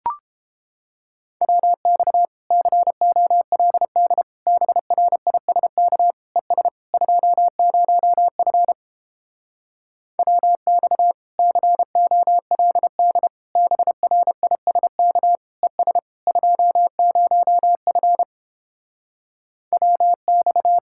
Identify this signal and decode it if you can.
Morse code, heard as a keyed tone, about 29 wpm, 700 Hz. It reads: WX COLD BRISK ES 20F WX COLD BRISK ES 20F WX